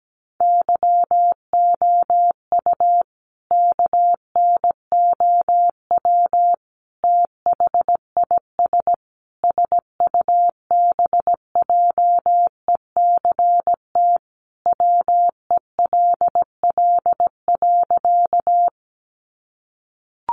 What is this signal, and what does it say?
Morse code, heard as a keyed tone, about 17 wpm, 700 Hz. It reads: YOU KNOW THIS SUBJECT WELL.
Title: Sentences from Top 800 Words 17wpm